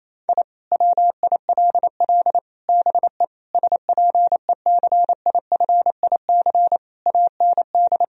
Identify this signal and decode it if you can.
Morse code, heard as a keyed tone, about 28 wpm, 700 Hz. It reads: I WILL BE SPECIFIC AND
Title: Sentences from Top 1000 Words - Review Part 3 - 28wpm